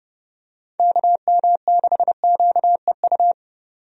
Morse code, heard as a keyed tone, about 30 wpm, 700 Hz. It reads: KM6QEU
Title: USA Call Signs - 6 Characters - 30wpm